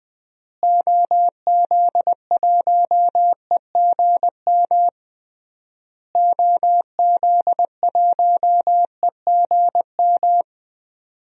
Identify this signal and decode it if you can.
Morse code, heard as a keyed tone, about 20 wpm, 700 Hz. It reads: OZ1EGM OZ1EGM
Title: DXCC-100 Call Signs - 6 Characters - Repeated 3x - 20wpm